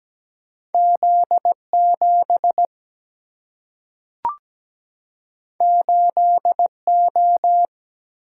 Morse code, heard as a keyed tone, about 17 wpm, 700 Hz. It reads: Z7 8O
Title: Single-Letter Single-Number 17wpm